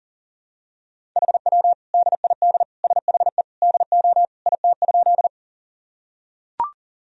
Morse code, heard as a keyed tone, about 40 wpm, 700 Hz. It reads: HW DID SHE DO IT?
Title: Sentences from Top 100 Words and Abbreviations 40wpm